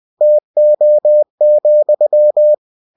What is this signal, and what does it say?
Morse code, heard as a keyed tone, about 20 wpm, 600 Hz. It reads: TO,
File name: cw-books / winnie-the-pooh / winnie-20wpm-page-0116